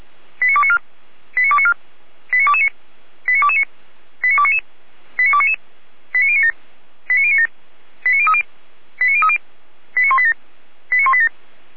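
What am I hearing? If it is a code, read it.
Five-tone selective call (ZVEI), heard as quick runs of five tones, tones of about 80 ms. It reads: 89294 89294 89229 89229 89299 89299 89097 89097 89030 89030 89187 89187
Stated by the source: ZVEI five-tone call received by SDR, fire brigade Orlamünde.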